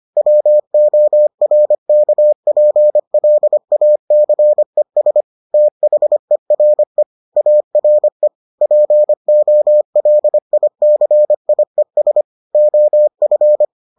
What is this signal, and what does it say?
Morse code, heard as a keyed tone, about 25 wpm, 600 Hz. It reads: WORKPLACES THERE ARE POLICIES OF